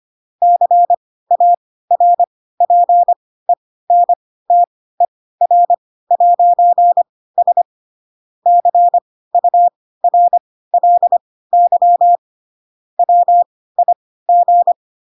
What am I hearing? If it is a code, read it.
Morse code, heard as a keyed tone, about 25 wpm, spaced as for 16 wpm, 700 Hz. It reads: CARPENTER'S CURLY WIG
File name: Pinocchio / Pinocchio ch2 10 25 ik1hge